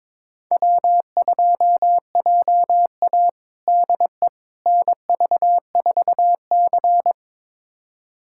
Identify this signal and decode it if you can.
Morse code, heard as a keyed tone, about 22 wpm, 700 Hz. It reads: W2JA DE NV4C
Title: USA Callsign DE Callsign - 4 Characters - 22wpm